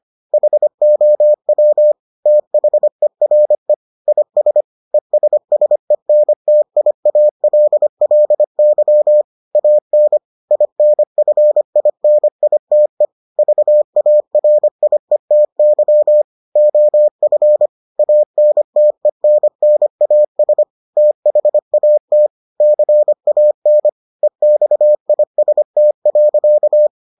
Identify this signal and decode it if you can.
Morse code, heard as a keyed tone, about 25 wpm, 600 Hz. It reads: HOW THERE IS ESSENTIALLY AN INFINITE VARIETY OF ANTENNAS THAT CAN EXIST.